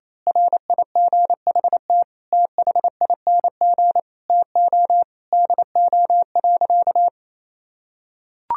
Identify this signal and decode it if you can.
Morse code, heard as a keyed tone, about 28 wpm, 700 Hz. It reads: RIGHT THING TO DO.
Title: Sentences from Top 1000 Words - Review Part 4 - 28wpm